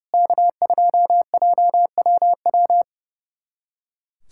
Morse code, heard as a keyed tone, about 30 wpm, 700 Hz. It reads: K2JWW
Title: SOTA QSO Practice 30wpm